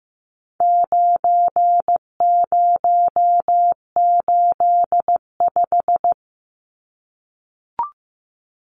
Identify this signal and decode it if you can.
Morse code, heard as a keyed tone, about 15 wpm, 700 Hz. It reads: 9085
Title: Numbers - 4-Digit 15wpm